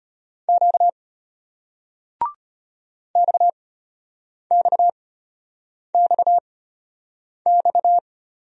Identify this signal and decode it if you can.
Morse code, heard as a keyed tone, about 34 wpm, spaced as for 8 wpm, 700 Hz. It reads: Q XXXX